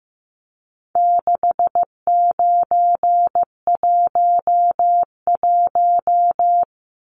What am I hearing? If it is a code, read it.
Morse code, heard as a keyed tone, about 15 wpm, 700 Hz. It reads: 6911
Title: Numbers - 4-Digit 15wpm